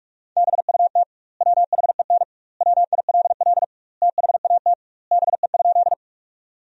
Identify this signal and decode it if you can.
Morse code, heard as a keyed tone, about 47 wpm, 700 Hz. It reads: BUT WHEN WILL THAT BE?
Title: 2x Word-Spacing - Sentences from Top 100 Words - 45wpm